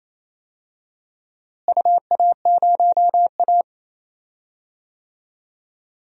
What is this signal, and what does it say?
Morse code, heard as a keyed tone, about 28 wpm, 700 Hz. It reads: UA0A
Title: DXCC-100 Call Signs - 4 Characters - Repeated 3x - 28wpm